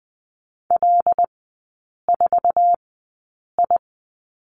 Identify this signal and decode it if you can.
Morse code, heard as a keyed tone, about 20 wpm, spaced as for 8 wpm, 700 Hz. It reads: L4I
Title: 100 Lesson 17e - Letter u - Three Characters 20wpm